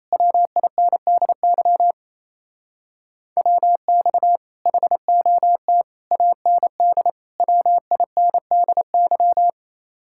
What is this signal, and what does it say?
Morse code, heard as a keyed tone, about 37 wpm, 700 Hz. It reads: WINDY WX HOT AND WINDY